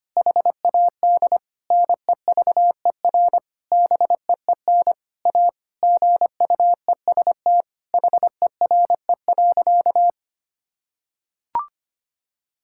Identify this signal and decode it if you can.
Morse code, heard as a keyed tone, about 25 wpm, 700 Hz. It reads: HAD NEVER BEEN A GUEST HERE.
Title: Sentences from Top 1600 Words 25wpm